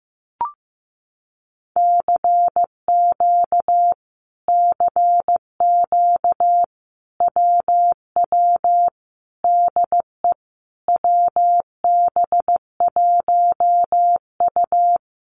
Morse code, heard as a keyed tone, about 15 wpm, 700 Hz. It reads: CQ CQ WW DE WB1U
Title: Common QSO Phrases Encore 15wpm